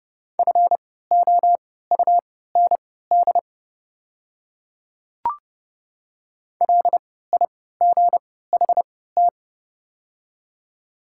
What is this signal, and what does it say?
Morse code, heard as a keyed tone, about 30 wpm, spaced as for 17 wpm, 690 Hz. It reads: FOUND LIGHT
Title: 3x Character Spacing - Top 500 Words - 30wpm